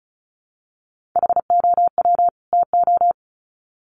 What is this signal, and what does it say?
Morse code, heard as a keyed tone, about 34 wpm, 700 Hz. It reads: HOW TO
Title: Sets of 2 Words 35wpm